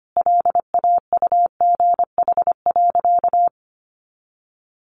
Morse code, heard as a keyed tone, about 25 wpm, 700 Hz. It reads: LAUGH.
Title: Sentences from Top 500 Words 25wpm